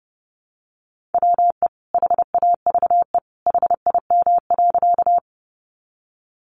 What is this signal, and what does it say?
Morse code, heard as a keyed tone, about 30 wpm, 700 Hz. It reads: WE HAVE HIM.
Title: Sentences from Top 100 Words 30wpm